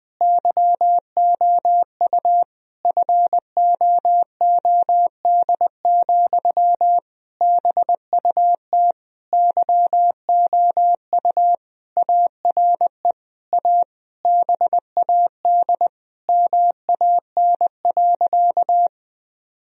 Morse code, heard as a keyed tone, about 20 wpm, 700 Hz. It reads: YOU FOOD, BUT YOU ARE A BAD MAN.
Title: Sentences from Top 1000 Words - Review Part 1 - 20wpm